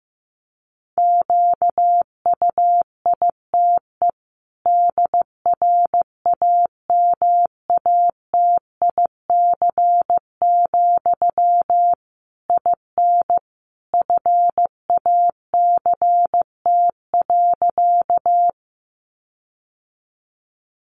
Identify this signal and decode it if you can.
Morse code, heard as a keyed tone, about 15 wpm, 700 Hz. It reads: QUITE DRAMATIC, IN FACT.
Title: Sentences from Top 1300 Words 15wpm